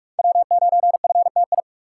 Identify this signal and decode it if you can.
Morse code, heard as a keyed tone, about 48 wpm, 700 Hz. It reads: W9FTI
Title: USA Call Signs - 5 Characters - 6x Character Spacing - 45wpm